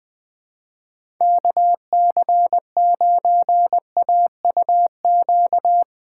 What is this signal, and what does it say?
Morse code, heard as a keyed tone, about 20 wpm, 700 Hz. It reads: KC9AUQ
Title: USA Callsign DE Callsign - 6 Characters - 20wpm